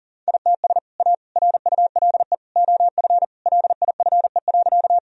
Morse code, heard as a keyed tone, about 40 wpm, 700 Hz. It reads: ITS A RULE OF LIFE.
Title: Sentences from Top 1000 Words - Review Part 2 - 40wpm